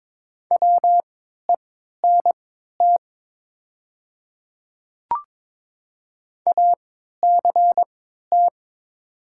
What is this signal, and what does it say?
Morse code, heard as a keyed tone, about 22 wpm, spaced as for 13 wpm, 700 Hz. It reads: WENT ACT
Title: 3x Character Spacing - Top 200 Words - 22wpm